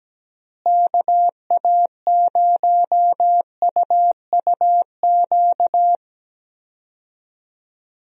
Morse code, heard as a keyed tone, about 17 wpm, 700 Hz. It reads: KA0UUQ
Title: USA Call Signs - 6 Characters - Repeated 3x - 17wpm